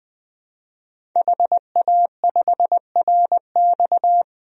Morse code, heard as a keyed tone, about 21 wpm, 700 Hz. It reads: HA5RX
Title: DXCC-100 Call Signs - 5 Characters - 20wpm